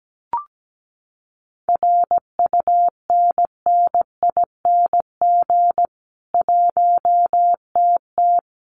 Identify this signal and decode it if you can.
Morse code, heard as a keyed tone, about 17 wpm, 700 Hz. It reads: RUNNING 1TT
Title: Common QSO Phrases 17wpm